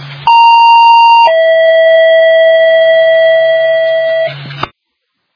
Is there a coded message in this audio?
Yes: two-tone page from 0.3 to 4.3 s, heard as a one-second tone, then a three-second tone, about 930 then 650 Hz, with noise.